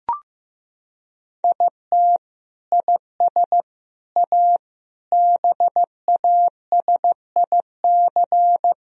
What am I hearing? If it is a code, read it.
Morse code, heard as a keyed tone, about 15 wpm, 700 Hz. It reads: IT IS A BASIC
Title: Sentences from Top 1000 Words - Review Part 3 - 15wpm